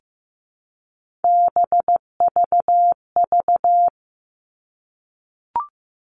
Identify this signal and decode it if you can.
Morse code, heard as a keyed tone, about 15 wpm, 700 Hz. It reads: BVV
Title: All Cut Numbers - 3-Digit 15wpm